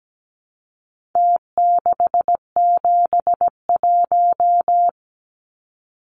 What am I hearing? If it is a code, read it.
Morse code, heard as a keyed tone, about 17 wpm, 700 Hz. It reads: T671
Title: Numbers - Contest Style - 4-Digit 17wpm